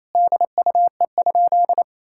Morse code, heard as a keyed tone, about 28 wpm, 700 Hz. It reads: DUE?
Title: Sentences from Top 1000 Words - Review Part 6 - 28wpm